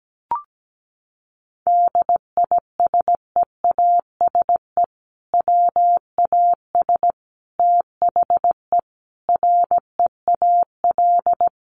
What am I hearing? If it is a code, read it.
Morse code, heard as a keyed tone, about 17 wpm, 700 Hz. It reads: DISEASE WAS THE REAL
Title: Sentences from Top 1100 Words 17wpm